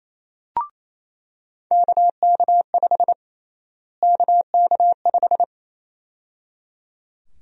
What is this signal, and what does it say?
Morse code, heard as a keyed tone, about 28 wpm, 700 Hz. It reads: KK5 KK5
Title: USA Call Sign Prefixes - Repeated 2x - 28wpm